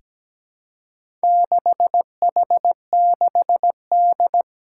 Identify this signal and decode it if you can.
Morse code, heard as a keyed tone, about 17 wpm, 700 Hz. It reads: 6H6D
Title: Four Challenging Letter-Number Combinations 17wpm